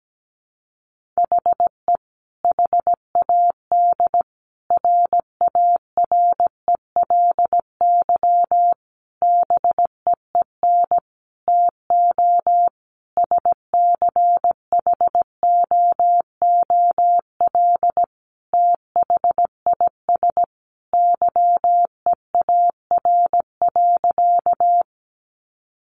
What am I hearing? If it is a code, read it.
Morse code, heard as a keyed tone, about 17 wpm, 700 Hz. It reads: HE HAD RARELY BEEN TO SCHOOL THIS YEAR.
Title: Sentences from Top 1400 Words 17wpm